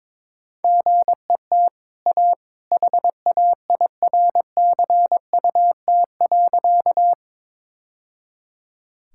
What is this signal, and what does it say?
Morse code, heard as a keyed tone, about 22 wpm, 690 Hz. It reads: GET A HAIRCUT.
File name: practice / 22wpm / Easy Sentences with Hints - 22wpm